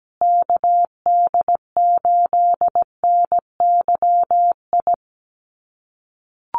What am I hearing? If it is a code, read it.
Morse code, heard as a keyed tone, about 17 wpm, 700 Hz. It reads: KD8NYI
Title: USA Call Signs Repeated 3x 17wpm